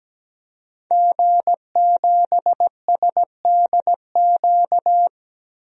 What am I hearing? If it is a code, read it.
Morse code, heard as a keyed tone, about 17 wpm, 700 Hz. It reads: G7SDQ
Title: DXCC-100 Call Signs - 5 Characters - Repeated 3x - 17wpm